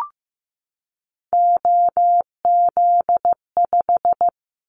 Morse code, heard as a keyed tone, about 15 wpm, 700 Hz. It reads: OZ5
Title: Callsign Prefixes for Top 50 DXCC - 15wpm